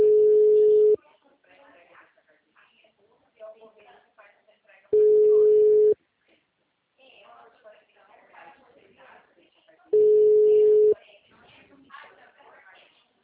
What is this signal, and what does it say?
Ringback tone (Europe, 425 Hz), with speech.
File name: out- -2024- - - .90398